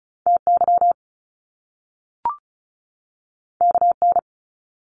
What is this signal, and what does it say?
Morse code, heard as a keyed tone, about 36 wpm, 700 Hz. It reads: TY KN
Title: Top Two Letter Combinations in Words 35wpm